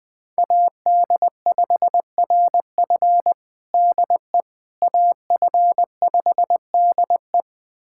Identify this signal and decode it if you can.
Morse code, heard as a keyed tone, about 21 wpm, 700 Hz. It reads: AD5RF DE AF5DE
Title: USA Callsign DE Callsign - 5 Characters - 20wpm